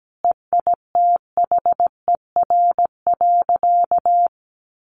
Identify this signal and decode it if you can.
Morse code, heard as a keyed tone, about 17 wpm, 700 Hz. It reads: EITHER.
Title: Sentences from Top 1000 Words 17wpm